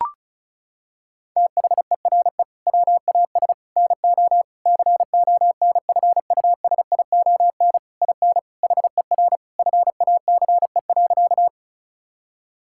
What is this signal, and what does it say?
Morse code, heard as a keyed tone, about 36 wpm, 700 Hz. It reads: THERE WAS NO CONFUSION IN HER FACE.
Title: Sentences from Top 1600 Words 35wpm